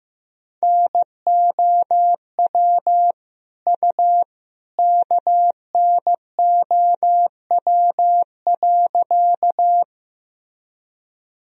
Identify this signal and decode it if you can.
Morse code, heard as a keyed tone, about 15 wpm, 700 Hz. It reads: NOW U KNOW.
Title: Sentences from Top 100 Words and Abbreviations 15wpm